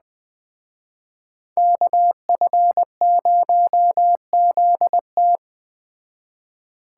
Morse code, heard as a keyed tone, about 20 wpm, 700 Hz. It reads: KF0ZT
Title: Speed Racing - USA 2x2 Call Signs 17wpm